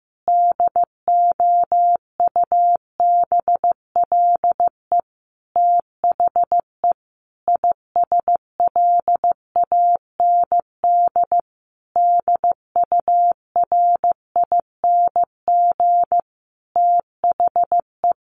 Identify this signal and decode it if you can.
Morse code, heard as a keyed tone, about 15 wpm, 700 Hz. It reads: DOUBLE THE ISLAND DURING THE
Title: Sentences from Top 1400 Words 15wpm